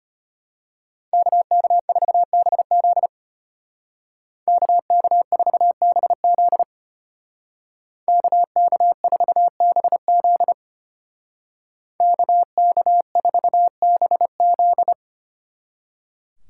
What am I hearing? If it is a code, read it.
Morse code, heard as a keyed tone, about 30 wpm, 700 Hz. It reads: KK4BZ KK4BZ KK4BZ KK4BZ
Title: Speed Racing - USA 2x2 Call Signs 25wpm